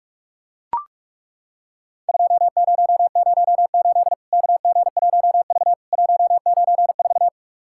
Morse code, heard as a keyed tone, about 41 wpm, 700 Hz. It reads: 2008 KG1V 194
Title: Worked All Europe DX Contest - QTC Exchange - 45wpm